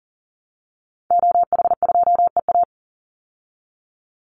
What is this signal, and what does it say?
Morse code, heard as a keyed tone, about 40 wpm, 700 Hz. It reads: OH2EA
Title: Call Signs Repeated 2x - 40wpm